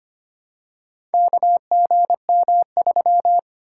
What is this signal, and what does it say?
Morse code, heard as a keyed tone, about 25 wpm, 690 Hz. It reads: KGM3
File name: practice / 25wpm / Four Letter-Number Combinations - 25wpm